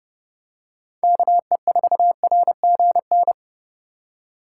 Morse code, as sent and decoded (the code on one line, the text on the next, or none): -.- . ....- .-. --. -.
KE4RGN